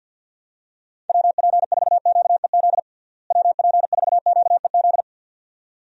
Morse code, heard as a keyed tone, about 49 wpm, 700 Hz. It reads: WP4QEZ WP4QEZ
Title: USA Call Signs - 6 Characters - Repeated 2x - 50wpm